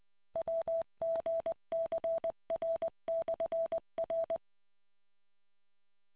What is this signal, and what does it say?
Morse code, heard as a keyed tone, about 20 wpm, 660 Hz. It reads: WGCR/R